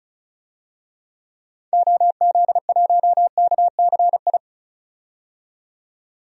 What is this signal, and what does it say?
Morse code, heard as a keyed tone, about 36 wpm, 700 Hz. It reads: OZ1KCI
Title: DXCC-100 Call Signs - 6 Characters - 35wpm